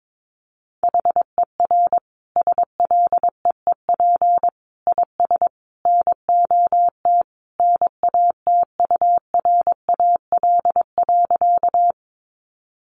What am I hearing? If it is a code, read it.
Morse code, heard as a keyed tone, about 22 wpm, 700 Hz. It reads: HER SLEEP IS NOT NATURAL.
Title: Sentences from Top 700 Words 22wpm